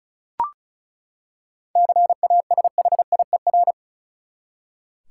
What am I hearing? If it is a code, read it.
Morse code, heard as a keyed tone, about 36 wpm, 700 Hz. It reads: CASHIER
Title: Practice QSO - Exchange 3 - More Info - 35wpm